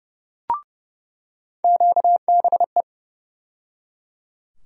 Morse code, heard as a keyed tone, about 29 wpm, 700 Hz. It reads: QBE
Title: USA Call Sign Postfixes - 3 Letters 30wpm